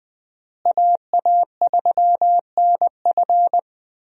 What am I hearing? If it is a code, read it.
Morse code, heard as a keyed tone, about 20 wpm, 700 Hz. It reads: AA3NF
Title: USA Call Signs - 5 Characters - Repeated 3x - 20wpm